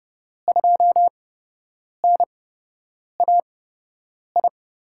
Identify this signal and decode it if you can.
Morse code, heard as a keyed tone, about 32 wpm, spaced as for 8 wpm, 700 Hz. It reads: 2NAI
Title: USA Call Signs - 6 Characters - 8x Character Spacing - 30wpm